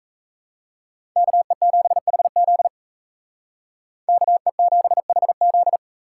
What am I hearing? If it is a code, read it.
Morse code, heard as a keyed tone, about 34 wpm, 700 Hz. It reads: KE7HZ KE7HZ
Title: Speed Racing - USA 2x2 Call Signs 28wpm